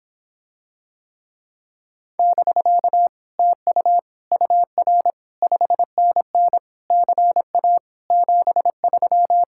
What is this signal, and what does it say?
Morse code, heard as a keyed tone, about 26 wpm, 700 Hz. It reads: <BK> TU UR 5NN CA 73